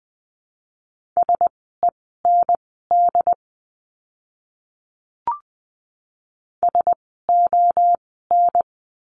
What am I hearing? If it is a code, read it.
Morse code, heard as a keyed tone, about 20 wpm, spaced as for 14 wpm, 700 Hz. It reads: SEND SON
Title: 101 Lesson 17f - Letter u - Words 20wpm